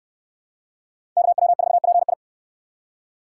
Morse code, heard as a keyed tone, about 63 wpm, 700 Hz. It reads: KW4PI